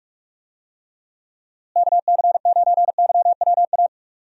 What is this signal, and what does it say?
Morse code, heard as a keyed tone, about 41 wpm, 700 Hz. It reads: KC9YWA
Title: USA Call Signs - 6 Characters - Repeated 2x - 45wpm